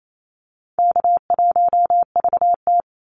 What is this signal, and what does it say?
Morse code, heard as a keyed tone, about 28 wpm, 700 Hz. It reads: K1VT
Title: Easy Call Signs Repeated 3x - 28wpm